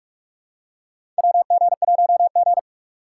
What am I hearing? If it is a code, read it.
Morse code, heard as a keyed tone, about 46 wpm, 700 Hz. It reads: WG1G
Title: USA Call Signs - 4 Characters - 4x Character Spacing - 45wpm